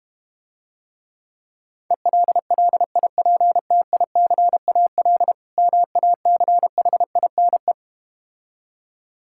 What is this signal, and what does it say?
Morse code, heard as a keyed tone, about 32 wpm, 700 Hz. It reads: ELLIPTICAL MACHINE